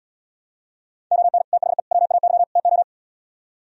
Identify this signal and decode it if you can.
Morse code, heard as a keyed tone, about 77 wpm, 700 Hz. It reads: ON THE WAY TO